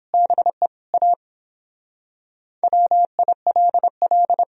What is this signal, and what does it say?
Morse code, heard as a keyed tone, about 26 wpm, 700 Hz. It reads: BE A WILL
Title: Speed-Racing - Sets of 3 Words 22wpm